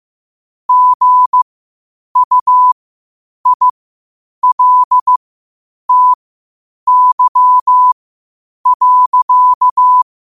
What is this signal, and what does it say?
Morse code, heard as a keyed tone, about 15 wpm, spaced as for 8 wpm, 1000 Hz. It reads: GUILTY.
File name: qotdfiles / 8wpm